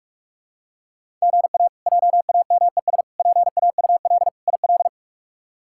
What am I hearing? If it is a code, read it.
Morse code, heard as a keyed tone, about 46 wpm, 700 Hz. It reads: GA JAMES PAUL IL
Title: K1USN SST QSO Practice - 45wpm